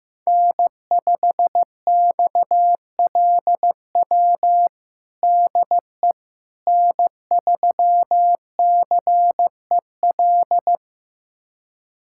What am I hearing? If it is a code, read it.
Morse code, heard as a keyed tone, about 15 wpm, 700 Hz. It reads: N5XLW DE N3CEL
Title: USA Callsign DE Callsign 15wpm